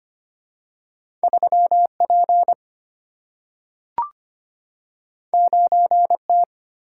Morse code, heard as a keyed tone, about 25 wpm, 700 Hz. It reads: 3P 9T
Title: Single-Letter Single-Number 25wpm